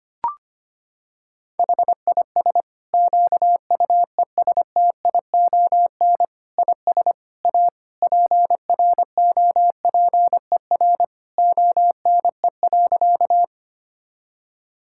Morse code, heard as a keyed tone, about 25 wpm, 700 Hz. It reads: HIS QUESTION IS A PROPER ONE.
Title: Sentences from Top 1000 Words - Review Part 5 - 25wpm